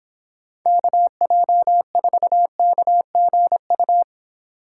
Morse code, heard as a keyed tone, about 25 wpm, 700 Hz. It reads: KJ4KGU